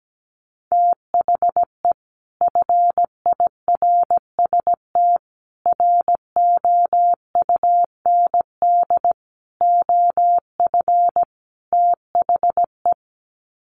Morse code, heard as a keyed tone, about 17 wpm, 700 Hz. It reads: THE FIRST ROUND OF THE
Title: Sets of 5 Words Encore 17wpm